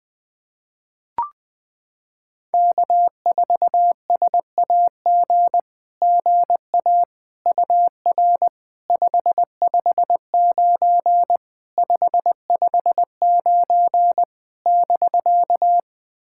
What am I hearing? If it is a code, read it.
Morse code, heard as a keyed tone, about 20 wpm, 700 Hz. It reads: K4SAG GA UR 559 559 <BK>
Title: SOTA QSO Practice 20wpm